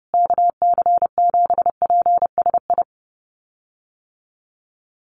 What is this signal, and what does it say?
Morse code, heard as a keyed tone, about 30 wpm, 700 Hz. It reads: KC7PSI
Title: USA Call Signs Repeated 3x 30wpm